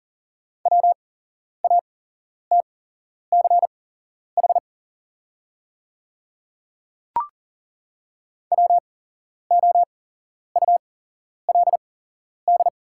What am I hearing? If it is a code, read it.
Morse code, heard as a keyed tone, about 40 wpm, spaced as for 11 wpm, 690 Hz. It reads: WATCH WOULD
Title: 8x Character Spacing - Top 300 Words - 40wpm